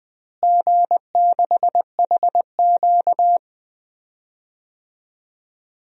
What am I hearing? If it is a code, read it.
Morse code, heard as a keyed tone, about 20 wpm, 700 Hz. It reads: G6HQ
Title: DXCC-100 Call Signs - 4 Characters - Repeated 3x - 20wpm